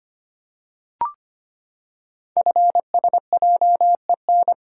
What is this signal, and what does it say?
Morse code, heard as a keyed tone, about 25 wpm, 700 Hz. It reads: FSJEN
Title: Code Groups - 5 Letters - 25wpm